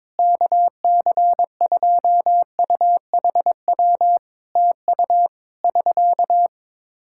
Morse code, heard as a keyed tone, about 22 wpm, 700 Hz. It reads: KC2UHW TU <SK>
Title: Practice QSO - Exchange 4 - Ending - 22wpm